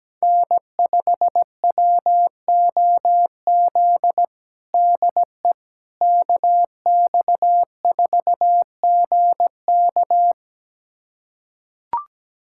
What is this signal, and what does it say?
Morse code, heard as a keyed tone, about 17 wpm, 700 Hz. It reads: N5WOZ DE KX4GK
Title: USA Callsign DE Callsign - 5 Characters - 17wpm